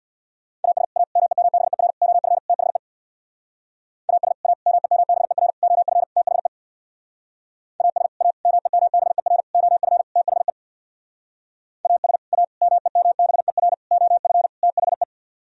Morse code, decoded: AS A MEMBER OF THE AS A MEMBER OF THE AS A MEMBER OF THE AS A MEMBER OF THE

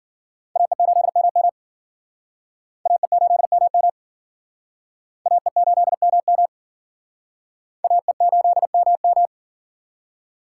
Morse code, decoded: AE8MM AE8MM AE8MM AE8MM

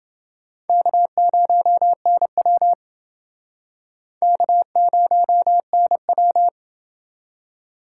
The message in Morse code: K0NW K0NW

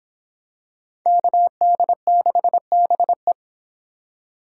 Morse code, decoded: KD6BE

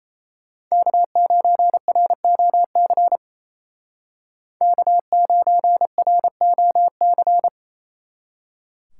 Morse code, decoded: K9ROC K9ROC

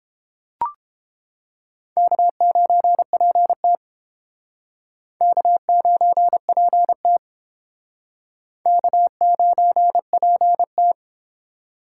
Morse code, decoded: K9PT K9PT K9PT

